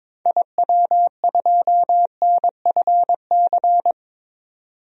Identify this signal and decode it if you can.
Morse code, heard as a keyed tone, about 21 wpm, 700 Hz. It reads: IW2NFC
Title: DXCC-100 Call Signs - 6 Characters - 22wpm